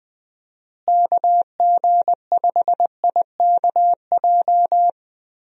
Morse code, heard as a keyed tone, about 21 wpm, 700 Hz. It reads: KG5IKJ
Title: USA Call Signs - 6 Characters - Repeated 2x - 20wpm